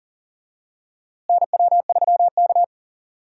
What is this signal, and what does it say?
Morse code, heard as a keyed tone, about 40 wpm, 700 Hz. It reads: NW3K
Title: USA Call Signs - 4 Characters - Repeated 2x - 40wpm